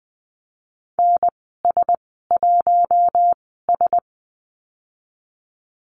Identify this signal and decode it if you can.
Morse code, heard as a keyed tone, about 20 wpm, spaced as for 15 wpm, 700 Hz. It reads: NS1S
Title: 042 Lesson 8g - Number 1 - Call Signs 20wpm